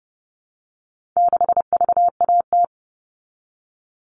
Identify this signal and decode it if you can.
Morse code, heard as a keyed tone, about 30 wpm, 700 Hz. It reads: 6VAT
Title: All Cut Numbers - 4-Digit 30wpm